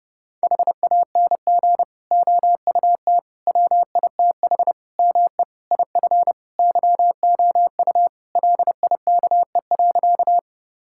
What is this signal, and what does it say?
Morse code, heard as a keyed tone, about 30 wpm, 700 Hz. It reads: HANG OUT WITH ME IF YOU LIKE.
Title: Sentences from Top 1600 Words 30wpm